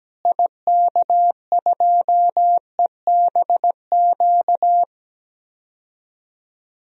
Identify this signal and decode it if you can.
Morse code, heard as a keyed tone, about 17 wpm, 700 Hz. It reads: IK2EBQ
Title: DXCC-100 Call Signs - 6 Characters - Repeated 3x - 17wpm